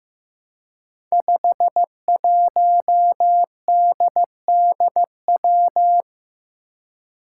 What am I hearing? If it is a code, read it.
Morse code, heard as a keyed tone, about 15 wpm, 700 Hz. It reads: 51DDW
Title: Code Groups - 5 Letter:Numbers - 15wpm